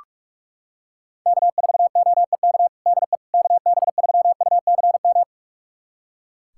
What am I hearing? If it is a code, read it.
Morse code, heard as a keyed tone, about 41 wpm, 700 Hz. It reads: K4OEK DE KB3UCM
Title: USA Callsign DE Callsign 45wpm